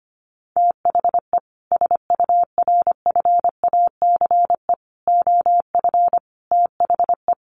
Morse code, heard as a keyed tone, about 25 wpm, 700 Hz. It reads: THE SURFACE OF THE
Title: Sets of 4 Words 25wpm